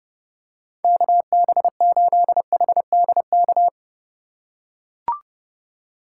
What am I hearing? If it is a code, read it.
Morse code, heard as a keyed tone, about 32 wpm, 700 Hz. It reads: KB8HDK
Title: USA Call Signs - 6 Characters - 2x Character Spacing - 30wpm